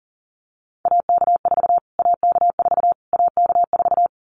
Morse code, heard as a keyed tone, about 40 wpm, 700 Hz. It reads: AK4 AK4 AK4
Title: Callsign Prefixes for Top 25 DXCC - Repeated 3x - 40wpm